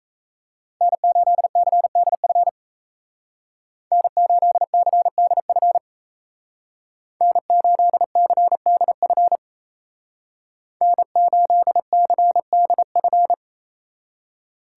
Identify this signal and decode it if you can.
Morse code, heard as a keyed tone, about 34 wpm, 700 Hz. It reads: N8CDF N8CDF N8CDF N8CDF
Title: Speed Racing - USA 1x3 Call Signs 28wpm